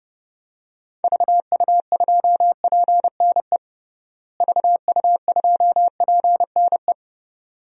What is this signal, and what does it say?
Morse code, heard as a keyed tone, about 30 wpm, 700 Hz. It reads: VU2PNE VU2PNE
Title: DXCC-100 Call Signs - 6 Characters - Repeated 2x - 30wpm